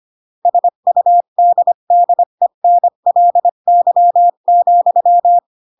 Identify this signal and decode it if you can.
Morse code, heard as a keyed tone, about 25 wpm, 700 Hz. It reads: SUDDENLY,
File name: Pinocchio / Pinocchio ch5 20 25 ik1hge